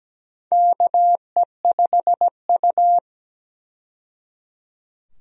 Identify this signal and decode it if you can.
Morse code, heard as a keyed tone, about 17 wpm, 700 Hz. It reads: KE5U
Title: USA Call Signs - 4 Characters - 17wpm